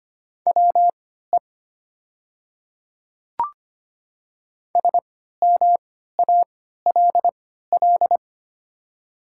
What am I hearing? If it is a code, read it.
Morse code, heard as a keyed tone, about 25 wpm, spaced as for 14 wpm, 700 Hz. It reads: WE SMALL
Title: 3x Character Spacing - Top 200 Words - 25wpm